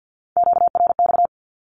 Morse code, heard as a keyed tone, about 50 wpm, 700 Hz. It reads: QRX
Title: Q-Codes with Spoken Meaning - 50wpm